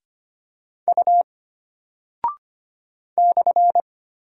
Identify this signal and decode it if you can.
Morse code, heard as a keyed tone, about 25 wpm, 700 Hz. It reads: U /
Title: Single Character (Letter:Number:Punctuation:Prosign) Alt 25wpm